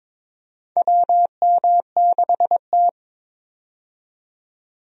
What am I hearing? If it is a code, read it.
Morse code, heard as a keyed tone, about 22 wpm, 700 Hz. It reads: WM6T
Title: USA Call Signs - 4 Characters - 22wpm